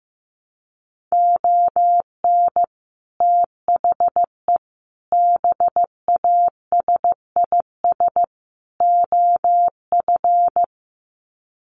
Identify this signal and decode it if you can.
Morse code, heard as a keyed tone, about 15 wpm, 700 Hz. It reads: ON THE BASIS OF
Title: Sets of 4 Words 15wpm